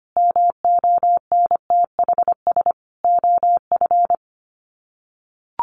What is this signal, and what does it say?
Morse code, heard as a keyed tone, about 25 wpm, 700 Hz. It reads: MONTHS OF
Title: Sets of 5 Words Encore2 25wpm